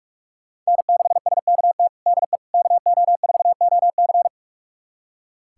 Morse code, heard as a keyed tone, about 46 wpm, 700 Hz. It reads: N6SKT DE KO4OC
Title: USA Callsign DE Callsign - 5 Characters - 45wpm